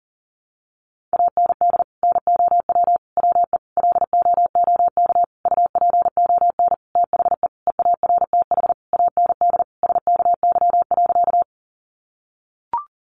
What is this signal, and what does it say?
Morse code, heard as a keyed tone, about 40 wpm, 700 Hz. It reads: AND NOW WE LOOK UPON THE EARTH AND SKY.
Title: Sentences from Top 1200 Words 40wpm